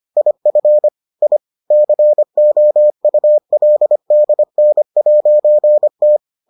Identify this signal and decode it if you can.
Morse code, heard as a keyed tone, about 25 wpm, 600 Hz. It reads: IF I COULDN'T